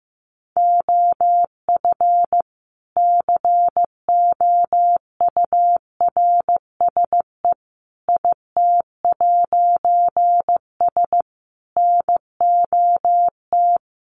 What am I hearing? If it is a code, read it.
Morse code, heard as a keyed tone, about 15 wpm, 700 Hz. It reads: OF COURSE IT'S NOT
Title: A Sound Of Thunder 15wpm